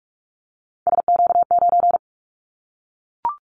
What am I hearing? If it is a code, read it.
Morse code, heard as a keyed tone, about 44 wpm, 700 Hz. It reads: SQ9
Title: Callsign Prefixes for Top 25 DXCC - Repeated 3x - 45wpm